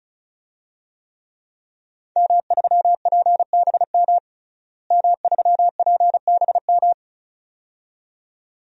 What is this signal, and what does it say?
Morse code, heard as a keyed tone, about 36 wpm, 700 Hz. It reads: M3PBM M3PBM
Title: DXCC-100 Call Signs - 5 Characters - Repeated 2x - 35wpm